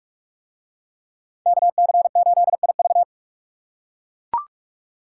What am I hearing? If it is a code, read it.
Morse code, heard as a keyed tone, about 40 wpm, 700 Hz. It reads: KC8IV